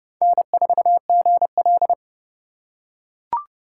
Morse code, heard as a keyed tone, about 26 wpm, 700 Hz. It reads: N4GL